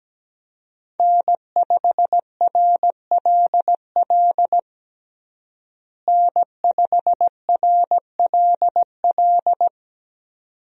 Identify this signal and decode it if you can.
Morse code, heard as a keyed tone, about 17 wpm, 700 Hz. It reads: N5RLL N5RLL
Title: USA Call Signs - 5 Characters - Repeated 2x - 17wpm